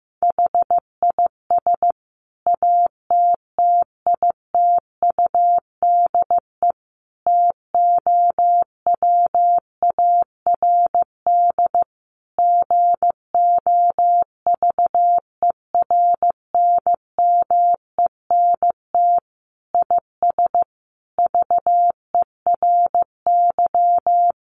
Morse code, heard as a keyed tone, about 15 wpm, 700 Hz. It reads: HIS ATTITUDE TOWARD GOVERNMENT IS VERY
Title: Sentences from Top 1300 Words 15wpm